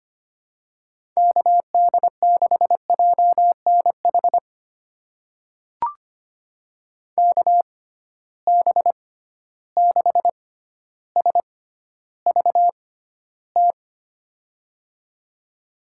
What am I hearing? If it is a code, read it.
Morse code, heard as a keyed tone, about 25 wpm, spaced as for 13 wpm, 700 Hz. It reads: KD6JNH KB6SVT
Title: USA Call Signs - 6 Characters - 6x Character Spacing - 25wpm